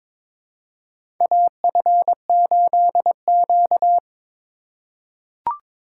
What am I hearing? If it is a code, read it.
Morse code, heard as a keyed tone, about 22 wpm, 700 Hz. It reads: AF8Q
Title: USA Call Signs - 4 Characters - 4x Character Spacing - 22wpm